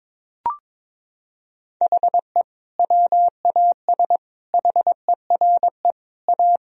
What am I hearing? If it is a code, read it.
Morse code, heard as a keyed tone, about 21 wpm, 700 Hz. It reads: HE WAS HERE A
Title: Sentences from Top 1000 Words - Review Part 3 - 22wpm